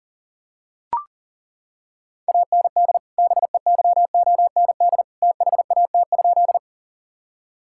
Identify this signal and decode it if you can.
Morse code, heard as a keyed tone, about 37 wpm, 700 Hz. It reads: AND BEYOND THAT?
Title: Sentences from Top 1000 Words - Review Part 6 - 40wpm